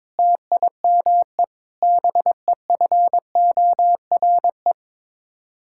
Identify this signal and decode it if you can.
Morse code, heard as a keyed tone, about 22 wpm, 700 Hz. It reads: TIME BEFORE
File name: practice / 22wpm / Sets of 6 Words - 22wpm